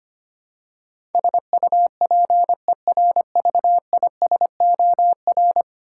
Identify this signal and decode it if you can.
Morse code, heard as a keyed tone, about 25 wpm, 690 Hz. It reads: SUPERVISOR
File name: 10 Letter Words - First 3 Letters Given - 25wpm